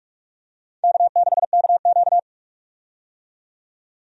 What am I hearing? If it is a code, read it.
Morse code, heard as a keyed tone, about 43 wpm, 700 Hz. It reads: K6KQ
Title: USA Call Signs - 4 Characters - 45wpm